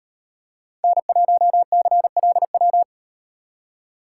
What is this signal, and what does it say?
Morse code, heard as a keyed tone, about 36 wpm, 700 Hz. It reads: N1CLW